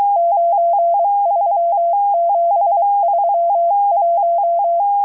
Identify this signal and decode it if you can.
Morse code, heard as a keyed tone, about 23 wpm, 680 Hz. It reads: 93731